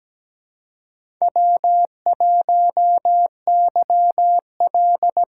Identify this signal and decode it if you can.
Morse code, heard as a keyed tone, about 17 wpm, 700 Hz. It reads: W1YL
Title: USA Call Signs - 4 Characters - 2x Character Spacing - 17wpm